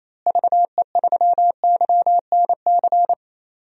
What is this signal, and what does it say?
Morse code, heard as a keyed tone, about 28 wpm, 700 Hz. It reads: VE3YNC
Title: DXCC-100 Call Signs - 6 Characters - Repeated 2x - 28wpm